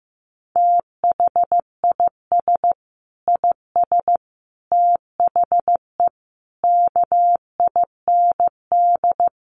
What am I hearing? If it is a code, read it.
Morse code, heard as a keyed tone, about 14 wpm, 700 Hz. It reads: THIS IS THE KIND
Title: Sets of 5 Words Encore 15wpm